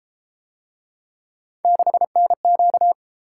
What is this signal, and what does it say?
Morse code, heard as a keyed tone, about 29 wpm, 700 Hz. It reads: 6NQ